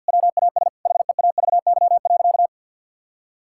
Morse code, heard as a keyed tone, about 50 wpm, 700 Hz. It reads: WAS HEAVY.
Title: Sentences from Top 1600 Words 50wpm